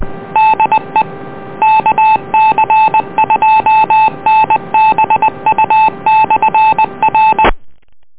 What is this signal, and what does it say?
Morse code, heard as a keyed tone, about 20 wpm, 870 Hz. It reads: DE KC2NBU/R
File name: net audio